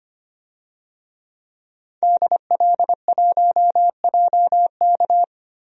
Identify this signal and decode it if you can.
Morse code, heard as a keyed tone, about 25 wpm, 700 Hz. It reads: DL1JK
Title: DXCC-100 Call Signs - 5 Characters - 25wpm